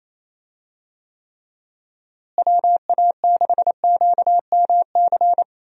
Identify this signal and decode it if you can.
Morse code, heard as a keyed tone, about 28 wpm, 700 Hz. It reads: WA6QMC
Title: USA Call Signs - 6 Characters - Repeated 2x - 28wpm